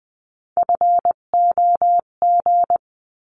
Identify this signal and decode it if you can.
Morse code, heard as a keyed tone, about 20 wpm, 700 Hz. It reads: FOG
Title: 258 Lesson 42c - Words 16-20wpm